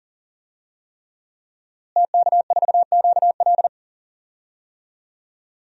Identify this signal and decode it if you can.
Morse code, heard as a keyed tone, about 40 wpm, 700 Hz. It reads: TK4QL
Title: DXCC-100 Call Signs - 5 Characters - Repeated 3x - 40wpm